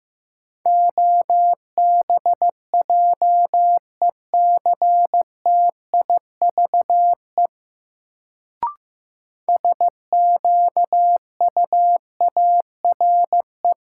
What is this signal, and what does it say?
Morse code, heard as a keyed tone, about 15 wpm, 700 Hz. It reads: OBJECTIVE SQUARE
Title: Sentences from Top 1200 Words - New Word Drill Alt - 15wpm